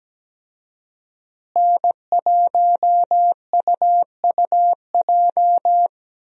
Morse code, heard as a keyed tone, about 17 wpm, 700 Hz. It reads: N1UUJ
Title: USA Call Signs - 5 Characters - Repeated 3x - 17wpm